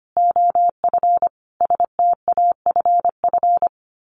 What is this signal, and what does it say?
Morse code, heard as a keyed tone, about 25 wpm, 700 Hz. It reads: OF STAFF
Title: Sets of 5 Words Encore 25wpm